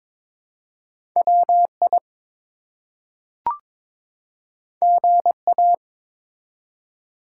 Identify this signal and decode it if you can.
Morse code, heard as a keyed tone, about 22 wpm, 700 Hz. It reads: WI GA
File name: ARRL Sections v2 - 22wpm